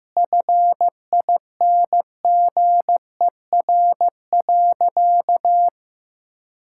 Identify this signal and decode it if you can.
Morse code, heard as a keyed tone, about 15 wpm, 700 Hz. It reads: FINGER.
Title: Sentences from Top 1600 Words 15wpm